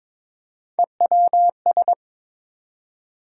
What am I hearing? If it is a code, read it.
Morse code, heard as a keyed tone, about 22 wpm, 700 Hz. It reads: EWS
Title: USA Call Sign Postfixes - 3 Letters 22wpm